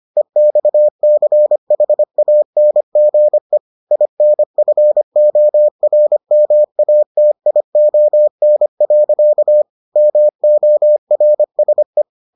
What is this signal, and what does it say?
Morse code, heard as a keyed tone, about 25 wpm, 600 Hz. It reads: EXCHANGE INFORMATION. MORSE